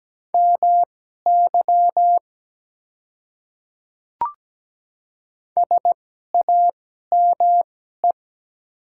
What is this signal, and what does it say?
Morse code, heard as a keyed tone, about 17 wpm, spaced as for 12 wpm, 700 Hz. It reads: MY SAME
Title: 2x Character Spacing - Top 300 Words - 17wpm